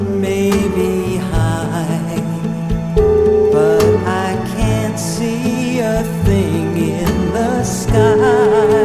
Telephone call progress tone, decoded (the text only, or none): ringback tone